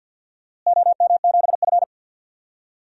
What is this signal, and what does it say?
Morse code, heard as a keyed tone, about 50 wpm, 700 Hz. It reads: OM7F
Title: DXCC-100 Call Signs - 4 Characters - Repeated 2x - 50wpm